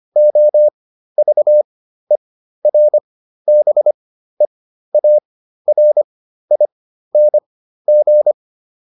Morse code, heard as a keyed tone, about 25 wpm, spaced as for 13 wpm, 600 Hz. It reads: OVERBEARING